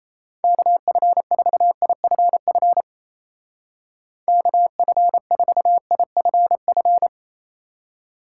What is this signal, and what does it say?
Morse code, heard as a keyed tone, about 32 wpm, 700 Hz. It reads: KF4IFF KF4IFF
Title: Speed Racing - USA Call Signs 28wpm